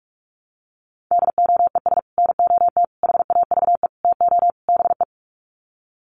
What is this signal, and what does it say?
Morse code, heard as a keyed tone, about 44 wpm, 700 Hz. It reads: DOES NOT HAVE TO BE
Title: Sets of 5 Words 45wpm